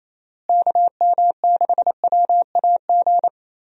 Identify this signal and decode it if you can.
Morse code, heard as a keyed tone, about 27 wpm, 700 Hz. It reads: KM6WAG
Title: USA Call Signs - 6 Characters - Repeated 3x - 28wpm